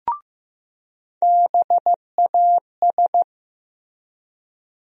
Morse code, heard as a keyed tone, about 15 wpm, 700 Hz. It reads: BAS
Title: USA Call Sign Postfixes - 3 Letters 15wpm